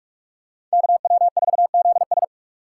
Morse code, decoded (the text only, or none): KW4ZS